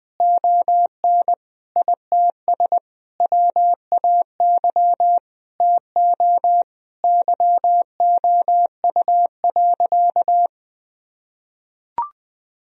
Morse code, heard as a keyed tone, about 20 wpm, 700 Hz. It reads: ON ITS WAY TO YOU.
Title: Sentences from Top 1000 Words - Review Part 1 - 20wpm